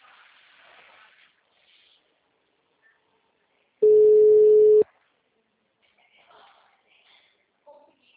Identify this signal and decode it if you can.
Ringback tone (Europe, 425 Hz), with speech.